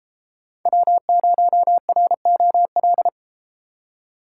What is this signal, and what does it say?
Morse code, heard as a keyed tone, about 32 wpm, 700 Hz. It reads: W0ROL